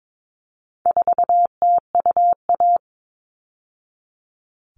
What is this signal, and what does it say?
Morse code, heard as a keyed tone, about 22 wpm, 700 Hz. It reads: 4TUA
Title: All Cut Numbers - 4-Digit 22wpm